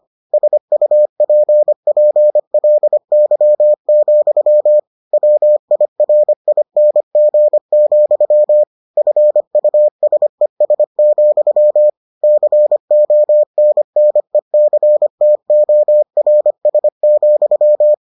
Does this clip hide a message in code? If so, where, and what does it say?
Yes: Morse code from 0.3 to 17.9 s, heard as a keyed tone, about 25 wpm, 600 Hz. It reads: SUPPLY, WIRING, FUSES, CONNECTORS,